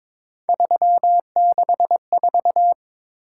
Morse code, heard as a keyed tone, about 22 wpm, 700 Hz. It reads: 364
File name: practice / 22wpm / Three Letter-Number Combinations - 22wpm